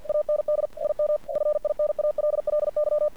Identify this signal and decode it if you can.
Morse code, heard as a keyed tone, about 47 wpm, 600 Hz. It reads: AND IM FINALLY
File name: archived / CRSnet